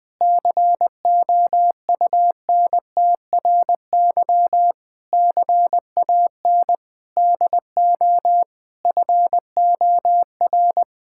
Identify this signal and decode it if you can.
Morse code, heard as a keyed tone, about 20 wpm, 700 Hz. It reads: COUNTRY CAN DO FOR
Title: Sets of 8 Words - 20wpm